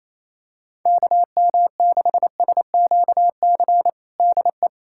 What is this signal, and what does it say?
Morse code, heard as a keyed tone, about 28 wpm, 700 Hz. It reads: KM6SQC DE
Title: USA Callsign DE Callsign - 6 Characters - 28wpm